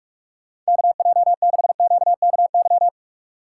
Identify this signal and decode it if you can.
Morse code, heard as a keyed tone, about 43 wpm, 690 Hz. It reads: KJ6QKY
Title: USA Call Signs - 8x Character Spacing - 45wpm